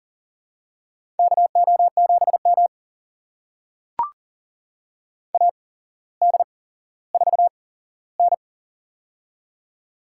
Morse code, heard as a keyed tone, about 42 wpm, spaced as for 17 wpm, 700 Hz. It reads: KO7M AD4N
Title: USA Call Signs - 4 Characters - 8x Character Spacing - 40wpm